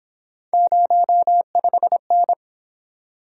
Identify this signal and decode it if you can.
Morse code, heard as a keyed tone, about 27 wpm, 700 Hz. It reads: 05N
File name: Speed-Racing - Three Letter-Number Combinations - 22wpm